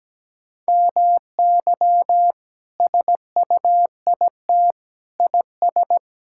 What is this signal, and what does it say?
Morse code, heard as a keyed tone, about 17 wpm, 700 Hz. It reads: MY SUIT IS
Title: Sentences from Top 1600 Words 17wpm